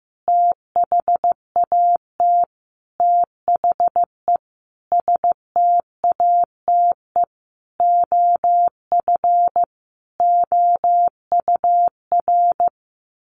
Morse code, heard as a keyed tone, about 15 wpm, 700 Hz. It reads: THAT THE STATE OF OUR
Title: Sentences from Top 1400 Words 15wpm